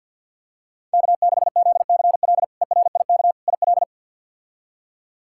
Morse code, heard as a keyed tone, about 47 wpm, 690 Hz. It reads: K6ZCL ERIK IL
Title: K1USN SST QSO Practice - 50wpm